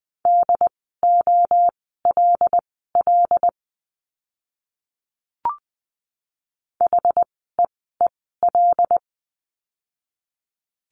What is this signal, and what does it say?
Morse code, heard as a keyed tone, about 20 wpm, spaced as for 14 wpm, 700 Hz. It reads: DOLL HEEL
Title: 075 Lesson 13f - Letter L - Words 20wpm